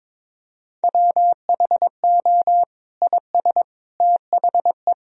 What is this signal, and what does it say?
Morse code, heard as a keyed tone, about 22 wpm, 700 Hz. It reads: WHO IS THE
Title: Sentences from Top 1000 Words - Review Part 3 - 22wpm